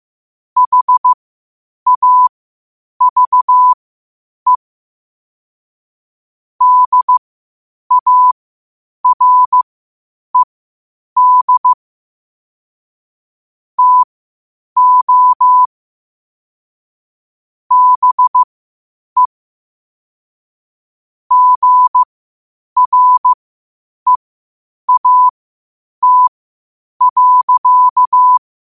Morse code, heard as a keyed tone, about 15 wpm, spaced as for 8 wpm, 1000 Hz. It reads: HAVE DARED TO BE GREAT.
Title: 8 WPM morse code quotes for Fri, 01 Mar 2024